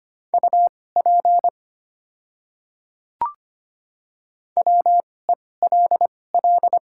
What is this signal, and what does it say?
Morse code, heard as a keyed tone, about 25 wpm, spaced as for 18 wpm, 690 Hz. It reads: UP WELL
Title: 2x Character Spacing - Top 200 Words - 25wpm